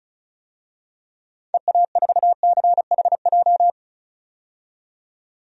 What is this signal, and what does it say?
Morse code, heard as a keyed tone, about 36 wpm, 700 Hz. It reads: EA4CHJ
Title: DXCC-100 Call Signs - 6 Characters - 35wpm